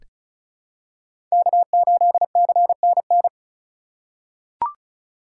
Morse code, heard as a keyed tone, about 34 wpm, 700 Hz. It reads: K8CNN
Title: USA Call Signs - 5 Characters - 2x Character Spacing - 35wpm